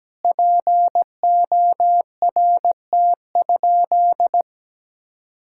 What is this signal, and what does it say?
Morse code, heard as a keyed tone, about 17 wpm, 700 Hz. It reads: PORT?
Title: Sentences from Top 1000 Words - Review Part 5 - 17wpm